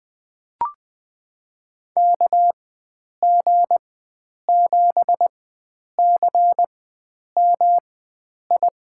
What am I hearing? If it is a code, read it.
Morse code, heard as a keyed tone, about 20 wpm, spaced as for 9 wpm, 700 Hz. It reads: KG7CMI
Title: USA Call Signs - 4x Character Spacing - 20wpm